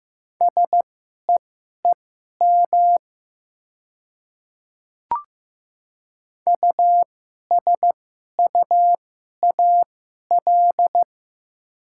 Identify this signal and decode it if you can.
Morse code, heard as a keyed tone, about 15 wpm, spaced as for 11 wpm, 700 Hz. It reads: SEEM USUAL
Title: 2x Character Spacing - Top 500 Words - 15wpm